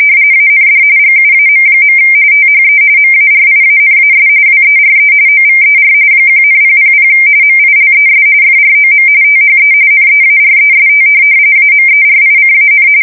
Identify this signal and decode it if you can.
RTTY, about 45.45 baud. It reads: AND GUAM. THE MARIANAS WERE OVER 1,000 MILES FROM THE NEAREST AMERICAN
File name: fsk170 / USS-New-Jersey